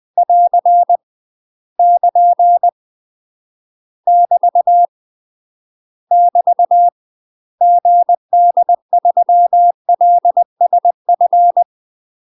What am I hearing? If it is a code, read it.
Morse code, heard as a keyed tone, about 20 wpm, 700 Hz. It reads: <AR> <KN> <BT> <BT> GD3LSF
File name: UK MASTER QSOs TRAINING / 20WPM / UK MASTER QSOs 0028